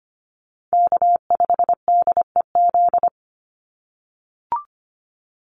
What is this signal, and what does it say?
Morse code, heard as a keyed tone, about 25 wpm, 700 Hz. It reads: K5DEZ
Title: NA SOTA Chaser Callsigns 25wpm